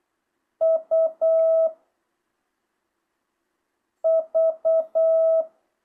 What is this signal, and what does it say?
Morse code, heard as a keyed tone, about 8 wpm, 650 Hz. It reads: U V
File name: jh8cbh